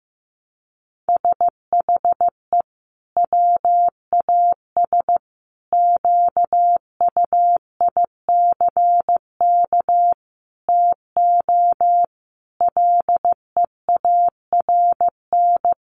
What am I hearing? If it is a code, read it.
Morse code, heard as a keyed tone, about 15 wpm, 700 Hz. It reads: SHE WAS QUICK TO LEARN
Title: Sentences from Top 1200 Words 15wpm